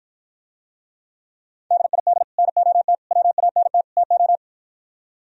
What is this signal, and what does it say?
Morse code, heard as a keyed tone, about 52 wpm, 700 Hz. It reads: DID NOT WANT TO